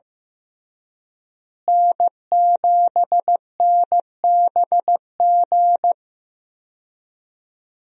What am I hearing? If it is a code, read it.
Morse code, heard as a keyed tone, about 15 wpm, 700 Hz. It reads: N7NBG
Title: USA Call Signs - 5 Characters - Repeated 2x - 15wpm